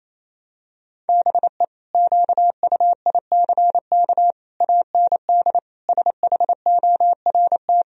Morse code, heard as a keyed tone, about 28 wpm, 700 Hz. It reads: BE QUICK AND SHORT
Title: Sentences from Top 1000 Words - Review Part 2 - 28wpm